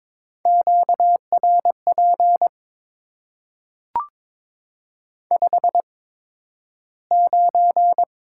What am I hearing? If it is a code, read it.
Morse code, heard as a keyed tone, about 22 wpm, spaced as for 12 wpm, 700 Hz. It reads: QRP 59
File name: Top 90 QSO Elements - 8x Character Spacing - 22wpm